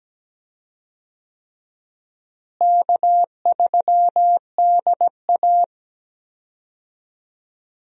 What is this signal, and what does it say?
Morse code, heard as a keyed tone, about 17 wpm, 700 Hz. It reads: K3DA
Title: USA Call Signs - 4 Characters - Repeated 3x - 17wpm